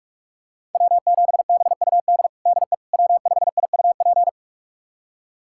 Morse code, heard as a keyed tone, about 47 wpm, 700 Hz. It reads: W7BUD DE W5IUP
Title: USA Callsign DE Callsign - 5 Characters - 45wpm